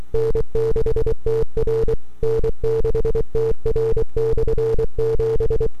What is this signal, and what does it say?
Morse code, heard as a keyed tone, about 24 wpm, 450 Hz. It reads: N6TR N6TR/7
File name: n6tr / 12Oct2009-160